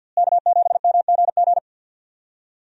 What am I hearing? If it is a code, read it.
Morse code, heard as a keyed tone, about 51 wpm, 700 Hz. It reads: K7MGG
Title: USA Call Signs - 5 Characters - Repeated 3x - 50wpm